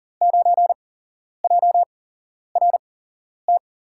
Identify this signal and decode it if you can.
Morse code, heard as a keyed tone, about 40 wpm, 700 Hz. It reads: 9 J R T
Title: USA Call Signs - 5 Characters - 8x Character Spacing - 40wpm